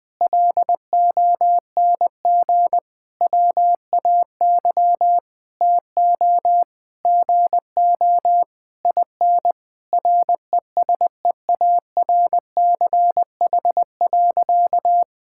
Morse code, heard as a keyed tone, about 21 wpm, 700 Hz. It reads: LONG WAY TO GO IN RESEARCH.
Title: Sentences with 9-Word Phrases - 20wpm